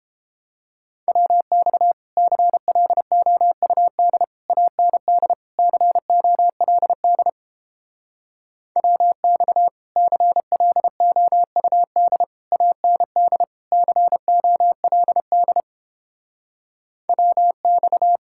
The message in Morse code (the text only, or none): WX CLOUD AND COLD WX CLOUD AND COLD WX